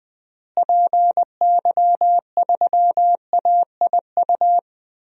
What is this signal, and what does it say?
Morse code, heard as a keyed tone, about 20 wpm, 700 Hz. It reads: PY3AIU
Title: DXCC-100 Call Signs - 6 Characters - Repeated 3x - 20wpm